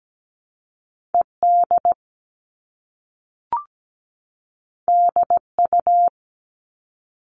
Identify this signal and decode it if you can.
Morse code, heard as a keyed tone, about 17 wpm, 700 Hz. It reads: ED DU
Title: All Cut Numbers - 2-Digit 17wpm